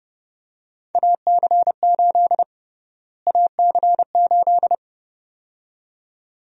Morse code, heard as a keyed tone, about 32 wpm, 700 Hz. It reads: AC8 AC8
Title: USA Call Sign Prefixes - Repeated 2x - 30wpm